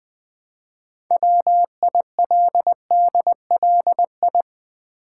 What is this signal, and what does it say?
Morse code, heard as a keyed tone, about 20 wpm, 700 Hz. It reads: WILDLI
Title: Build A Word - 3 Letters At A Time - 8-Letter Words - 20wpm